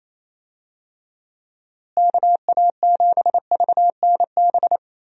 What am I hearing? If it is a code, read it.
Morse code, heard as a keyed tone, about 28 wpm, 700 Hz. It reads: KA7VNB
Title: USA Call Signs - 6 Characters - Repeated 3x - 28wpm